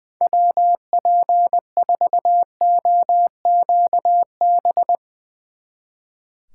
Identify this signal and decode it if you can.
Morse code, heard as a keyed tone, about 21 wpm, 700 Hz. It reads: WP4OQB
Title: Speed Racing - USA 2x3 Call Signs 20wpm